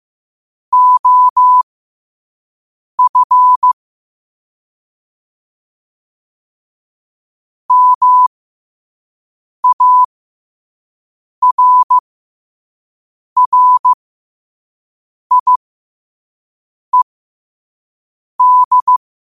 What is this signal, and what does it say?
Morse code, heard as a keyed tone, about 15 wpm, spaced as for 5 wpm, 1000 Hz. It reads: OF MARRIED
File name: qotdfiles / 5wpm